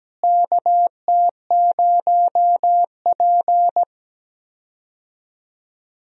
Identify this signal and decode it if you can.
Morse code, heard as a keyed tone, about 17 wpm, 700 Hz. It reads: KT0P
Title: USA Call Signs - 4 Characters - Repeated 3x - 17wpm